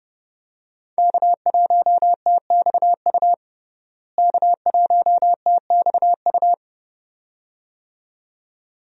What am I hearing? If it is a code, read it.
Morse code, heard as a keyed tone, about 30 wpm, 700 Hz. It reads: K1TXU K1TXU
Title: USA Call Signs - 5 Characters - Repeated 2x - 30wpm